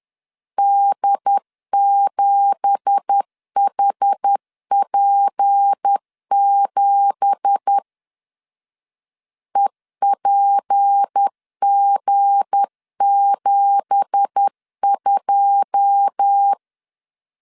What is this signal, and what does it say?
Morse code, heard as a keyed tone, about 10 wpm, 800 Hz. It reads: D7HP7 EPG72
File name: LEZIONI / 9 2867 10wpm 2733